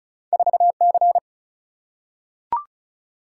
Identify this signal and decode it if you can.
Morse code, heard as a keyed tone, about 36 wpm, 700 Hz. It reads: 4C
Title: Single-Letter Single-Number 35wpm